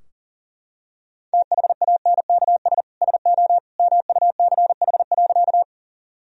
Morse code, heard as a keyed tone, about 40 wpm, 700 Hz. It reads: THANKS SO MUCH.
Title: Sentences from Top 1000 Words - Review Part 3 - 40wpm